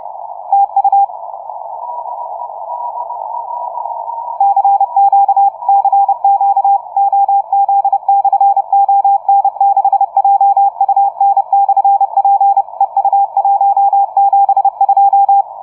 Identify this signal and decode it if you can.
Morse code, heard as a keyed tone, about 30 wpm, 800 Hz. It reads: TU CQ CQ OZ/ON6JUN/P EU172
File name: on6jun / OZ ON6JUN-CW-30m